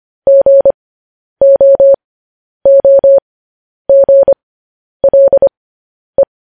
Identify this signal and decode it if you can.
Morse code, heard as a keyed tone, about 25 wpm, spaced as for 10 wpm, 550 Hz. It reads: GOOGLE
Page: news-headlines-in-morse-code-at-10-wpm